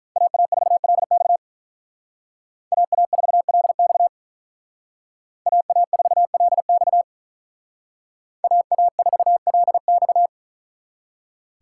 Morse code, heard as a keyed tone, about 44 wpm, 700 Hz. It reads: AA4LX AA4LX AA4LX AA4LX
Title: Speed Racing - USA 2x2 Call Signs 35wpm